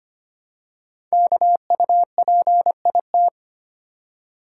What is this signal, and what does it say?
Morse code, heard as a keyed tone, about 25 wpm, 700 Hz. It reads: KUPIT
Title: Code Groups - 5 Letter:Numbers - 25wpm